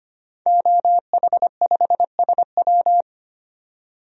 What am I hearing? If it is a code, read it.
Morse code, heard as a keyed tone, about 25 wpm, 700 Hz. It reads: OH5SW
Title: DXCC-100 Call Signs - 5 Characters - 25wpm